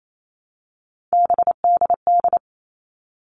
Morse code, heard as a keyed tone, about 28 wpm, 700 Hz. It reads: BDD
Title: All Cut Numbers - 3-Digit 28wpm